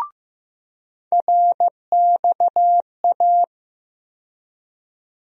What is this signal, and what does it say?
Morse code, heard as a keyed tone, about 15 wpm, 700 Hz. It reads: RXA
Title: USA Call Sign Postfixes - 3 Letters 15wpm